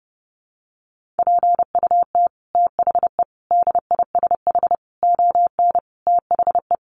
Morse code, heard as a keyed tone, about 28 wpm, 700 Hz. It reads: PUT THE DISH ON THE
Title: Sentences from Top 1300 Words 30wpm